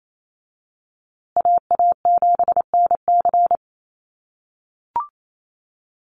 Morse code, heard as a keyed tone, about 28 wpm, 700 Hz. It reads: AA7NC
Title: USA Call Signs Repeated 3x 28wpm